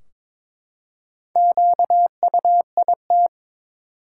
Morse code, heard as a keyed tone, about 22 wpm, 700 Hz. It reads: QUIT
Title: Four Letter Words - 22wpm